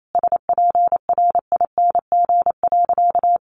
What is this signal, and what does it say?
Morse code, heard as a keyed tone, about 28 wpm, 700 Hz. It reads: SPRING.
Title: Sentences from Top 1200 Words 28wpm